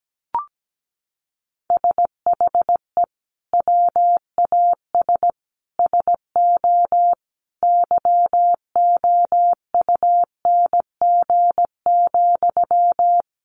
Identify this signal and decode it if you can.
Morse code, heard as a keyed tone, about 17 wpm, 700 Hz. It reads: SHE WAS SO YOUNG,
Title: Sentences from Top 400 Words 17wpm